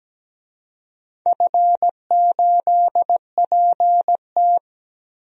Morse code, heard as a keyed tone, about 17 wpm, 700 Hz. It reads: F8PT
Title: DXCC-100 Call Signs - 4 Characters - Repeated 3x - 17wpm